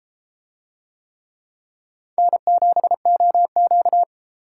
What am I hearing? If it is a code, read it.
Morse code, heard as a keyed tone, about 31 wpm, 700 Hz. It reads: N7OQ